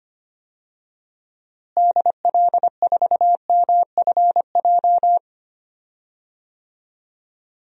Morse code, heard as a keyed tone, about 25 wpm, 700 Hz. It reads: DL4MFJ
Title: DXCC-100 Call Signs - 6 Characters - Repeated 3x - 25wpm